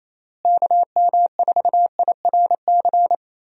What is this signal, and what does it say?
Morse code, heard as a keyed tone, about 28 wpm, 700 Hz. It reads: KM4IRC
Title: USA Call Signs - 6 Characters - Repeated 3x - 28wpm